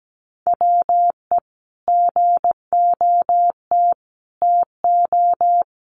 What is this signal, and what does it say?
Morse code, heard as a keyed tone, about 17 wpm, 700 Hz. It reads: WE GOT TO
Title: Sentences from Top 300 Words 17wpm